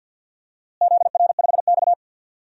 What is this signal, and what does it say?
Morse code, heard as a keyed tone, about 50 wpm, 700 Hz. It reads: ZR5X
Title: DXCC-100 Call Signs - 4 Characters - Repeated 3x - 50wpm